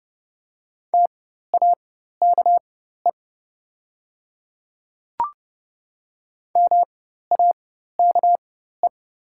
Morse code, heard as a keyed tone, about 30 wpm, spaced as for 14 wpm, 700 Hz. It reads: TAKE MAKE
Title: 4x Character Spacing - Top 300 Words - 30wpm